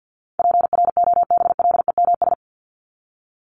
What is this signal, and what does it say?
Morse code, heard as a keyed tone, about 50 wpm, 700 Hz. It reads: PROBLEMS
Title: ES:S Suffix - Top 100 Words with Base Words 50wpm